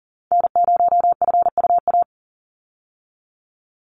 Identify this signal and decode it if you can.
Morse code, heard as a keyed tone, about 38 wpm, 700 Hz. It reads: N0FUA
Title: USA Call Signs 40wpm